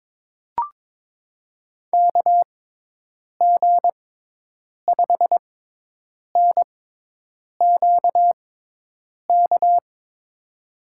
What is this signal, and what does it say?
Morse code, heard as a keyed tone, about 22 wpm, spaced as for 8 wpm, 700 Hz. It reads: KG5NQK
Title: USA Call Signs - 6 Characters - 6x Character Spacing - 22wpm